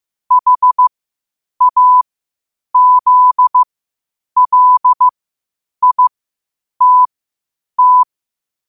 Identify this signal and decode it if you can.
Morse code, heard as a keyed tone, about 15 wpm, spaced as for 8 wpm, 1000 Hz. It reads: HAZLITT
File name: qotdfiles / 8wpm